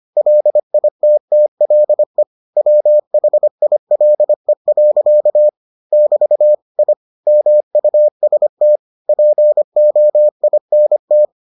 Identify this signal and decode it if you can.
Morse code, heard as a keyed tone, about 25 wpm, 600 Hz. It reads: LITTLE WHILE. <BT> I MUST POINT